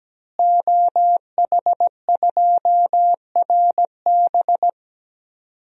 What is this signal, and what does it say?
Morse code, heard as a keyed tone, about 17 wpm, 700 Hz. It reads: OH2RB
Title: DXCC-100 Call Signs - 5 Characters - Repeated 2x - 17wpm